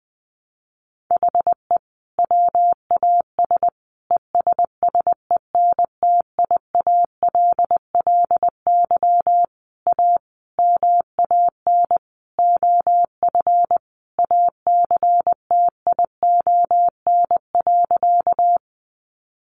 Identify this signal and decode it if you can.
Morse code, heard as a keyed tone, about 20 wpm, 700 Hz. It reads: HE WAS ESSENTIALLY A MAN OF ACTION.
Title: Sentences from Top 1300 Words 20wpm